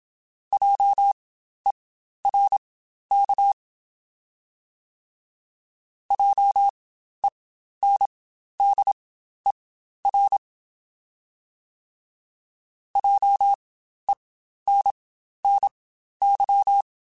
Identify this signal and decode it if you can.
Morse code, heard as a keyed tone, about 27 wpm, spaced as for 12 wpm, 800 Hz. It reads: JERK JENDER JENNY
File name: Spaced Alphas / Spaced Alphas mp3s / Js sp @25